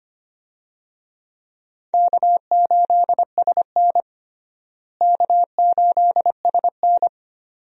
Morse code, heard as a keyed tone, about 25 wpm, 700 Hz. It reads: K8SN K8SN
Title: USA Call Signs - 4 Characters - Repeated 2x - 25wpm